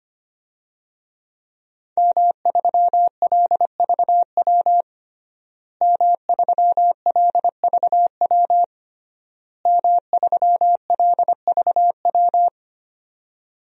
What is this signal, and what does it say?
Morse code, heard as a keyed tone, about 25 wpm, 700 Hz. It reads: M3LVW M3LVW M3LVW
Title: DXCC-100 Call Signs - 5 Characters - Repeated 3x - 25wpm